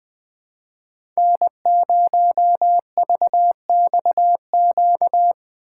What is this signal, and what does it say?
Morse code, heard as a keyed tone, about 20 wpm, 700 Hz. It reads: N0VXQ
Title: USA Call Signs - 5 Characters - 4x Character Spacing - 20wpm